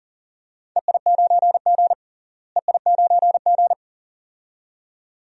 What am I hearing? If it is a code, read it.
Morse code, heard as a keyed tone, about 40 wpm, 700 Hz. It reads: EI9G EI9G
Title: DXCC-100 Call Signs - 4 Characters - Repeated 2x - 40wpm